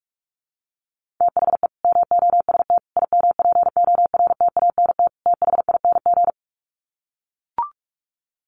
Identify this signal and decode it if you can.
Morse code, heard as a keyed tone, about 44 wpm, 700 Hz. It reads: THE MOST IMPORTANT THING
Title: Sets of 4 Words 45wpm